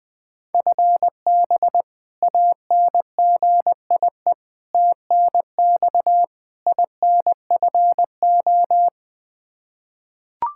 Morse code, heard as a keyed tone, about 20 wpm, 700 Hz. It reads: FB ANGIE TNX INFO
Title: K1USN SST QSO Practice 20wpm